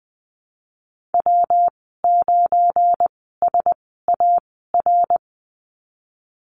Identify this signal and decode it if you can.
Morse code, heard as a keyed tone, about 19 wpm, spaced as for 14 wpm, 700 Hz. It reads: W9SAR
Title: 173 Lesson 28g - Number 9 - Call Signs 20wpm